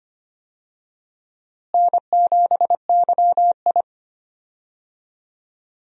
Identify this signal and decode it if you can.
Morse code, heard as a keyed tone, about 25 wpm, 700 Hz. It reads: N7YI
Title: USA Call Signs - 4 Characters - Repeated 2x - 25wpm